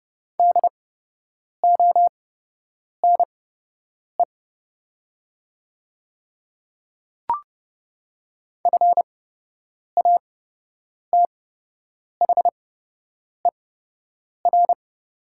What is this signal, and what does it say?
Morse code, heard as a keyed tone, about 30 wpm, spaced as for 8 wpm, 700 Hz. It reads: DONE FATHER
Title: 8x Character Spacing - Top 500 Words - 30wpm